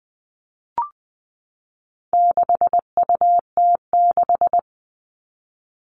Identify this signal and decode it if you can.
Morse code, heard as a keyed tone, about 20 wpm, 700 Hz. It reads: 6UT6
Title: All Cut Numbers - 4-Digit 20wpm